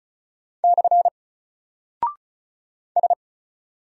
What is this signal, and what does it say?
Morse code, heard as a keyed tone, about 36 wpm, 700 Hz. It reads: / S
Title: Single Character (Letter:Number:Punctuation:Prosign) Alt 35wpm